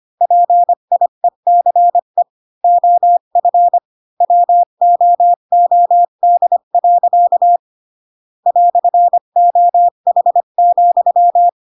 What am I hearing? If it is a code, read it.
Morse code, heard as a keyed tone, about 25 wpm, 700 Hz. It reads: PIECE OF WOOD. "OH,
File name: Pinocchio / Pinocchio ch1 20 25 ik1hge